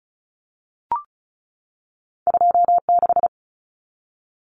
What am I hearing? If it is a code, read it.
Morse code, heard as a keyed tone, about 36 wpm, 700 Hz. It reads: 26
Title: Numbers - 2-Digit 35wpm